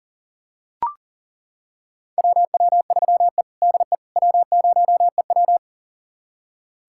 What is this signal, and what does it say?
Morse code, heard as a keyed tone, about 40 wpm, 700 Hz. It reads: WW3E DE W0EW
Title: USA Callsign DE Callsign - 4 Characters - 40wpm